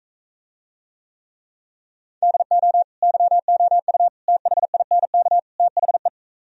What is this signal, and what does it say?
Morse code, heard as a keyed tone, about 44 wpm, 700 Hz. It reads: DO YOU THINK THE